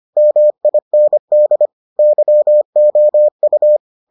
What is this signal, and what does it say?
Morse code, heard as a keyed tone, about 25 wpm, 600 Hz. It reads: MIND YOU